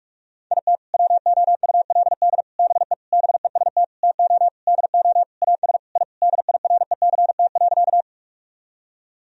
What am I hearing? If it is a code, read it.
Morse code, heard as a keyed tone, about 46 wpm, 700 Hz. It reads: IT WOULD BE BEST TO DO AS I DIRECT.
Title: Sentences from Top 1000 Words - Review Part 2 - 45wpm